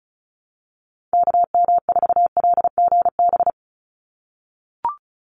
Morse code, heard as a keyed tone, about 34 wpm, 700 Hz. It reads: KM4LGB
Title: USA Call Signs 35wpm